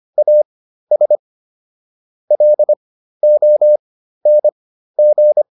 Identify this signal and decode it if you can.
Morse code, heard as a keyed tone, about 25 wpm, spaced as for 13 wpm, 600 Hz. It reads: AS LONG